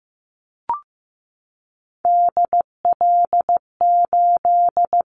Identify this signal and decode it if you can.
Morse code, heard as a keyed tone, about 15 wpm, 700 Hz. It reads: DL8
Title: Callsign Prefixes for Top 50 DXCC - 15wpm